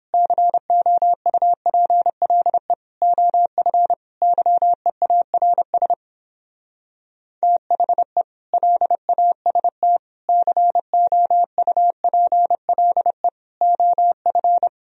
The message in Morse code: COUPLE OF YEARS THE LAST COUPLE OF

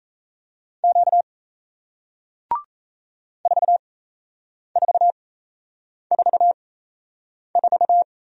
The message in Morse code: Q 4444

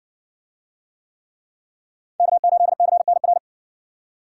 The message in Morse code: K8CNR

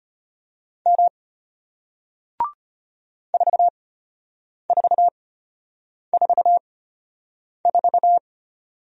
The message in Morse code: M 4444

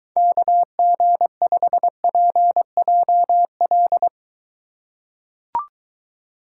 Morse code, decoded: KG5PJL

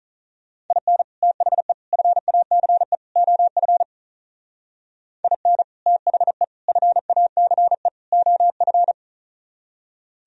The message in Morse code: IN THE FACE OF IN THE FACE OF

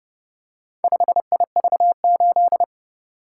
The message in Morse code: 5IV8

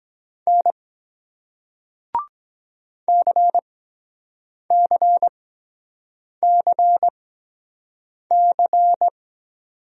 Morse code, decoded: N CCCC

